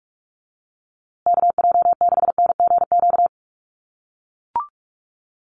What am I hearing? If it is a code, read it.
Morse code, heard as a keyed tone, about 44 wpm, 700 Hz. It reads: KJ6NGQ
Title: USA Call Signs Repeated 2x 45wpm